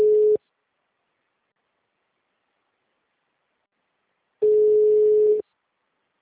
Ringback tone (Europe, 425 Hz).